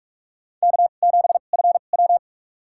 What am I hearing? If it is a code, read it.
Morse code, heard as a keyed tone, about 43 wpm, spaced as for 32 wpm, 700 Hz. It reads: K7FW
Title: USA Call Signs - 4 Characters - 2x Character Spacing - 45wpm